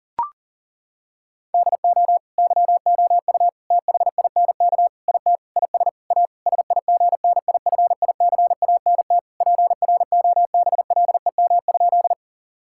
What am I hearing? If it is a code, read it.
Morse code, heard as a keyed tone, about 40 wpm, 700 Hz. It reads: DO YOU THINK IT IS A SIGNIFICANT PROBLEM?
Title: Sentences from Top 1000 Words - Review Part 6 - 40wpm